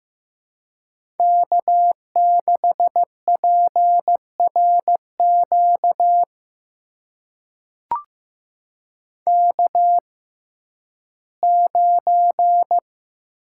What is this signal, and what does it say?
Morse code, heard as a keyed tone, about 15 wpm, spaced as for 11 wpm, 700 Hz. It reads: K6PRQ K9
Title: USA Call Signs - 5 Characters - 6x Character Spacing - 15wpm